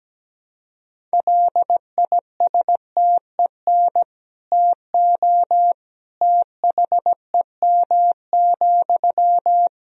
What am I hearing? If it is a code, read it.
Morse code, heard as a keyed tone, about 17 wpm, 700 Hz. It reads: LISTEN TO THEM,
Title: Sentences from Top 400 Words and Abbreviations 17wpm